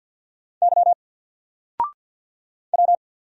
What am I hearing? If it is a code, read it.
Morse code, heard as a keyed tone, about 50 wpm, 700 Hz. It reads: Y W
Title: Single Letter-Number Alt 50wpm